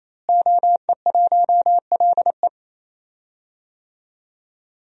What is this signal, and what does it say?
Morse code, heard as a keyed tone, about 27 wpm, 700 Hz. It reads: OE1LE
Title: DXCC-100 Call Signs - 5 Characters - Repeated 3x - 28wpm